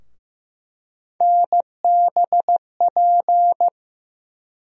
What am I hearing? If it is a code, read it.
Morse code, heard as a keyed tone, about 15 wpm, 700 Hz. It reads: NBP
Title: USA Call Sign Postfixes - 3 Letters 15wpm